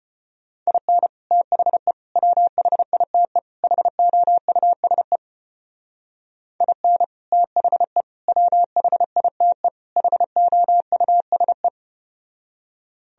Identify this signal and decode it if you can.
Morse code, heard as a keyed tone, about 33 wpm, 700 Hz. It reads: IN THE WHITE HOUSE IN THE WHITE HOUSE